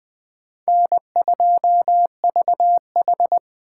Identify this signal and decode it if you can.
Morse code, heard as a keyed tone, about 20 wpm, 700 Hz. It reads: N2VH
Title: USA Call Signs - 4 Characters - 2x Character Spacing - 20wpm